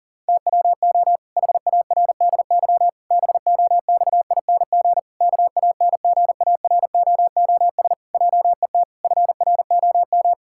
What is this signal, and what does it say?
Morse code, heard as a keyed tone, about 40 wpm, 700 Hz. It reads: TWO HARDY BOXING KANGAROOS JET FROM
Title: Pangram Challenge v2 - 40wpm